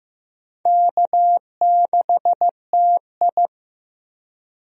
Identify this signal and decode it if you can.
Morse code, heard as a keyed tone, about 15 wpm, 700 Hz. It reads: K6TI
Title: USA Call Signs - 4 Characters - Repeated 2x - 15wpm